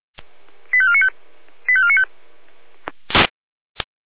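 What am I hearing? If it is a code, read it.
Five-tone selective call (ZVEI), heard as quick runs of five tones, tones of about 70 ms. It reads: 85495 85495
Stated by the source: ZVEI five-tone selective call received by SDR.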